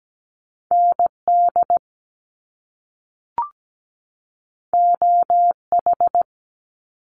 Morse code, heard as a keyed tone, about 17 wpm, 700 Hz. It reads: ND OH
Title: US States 17wpm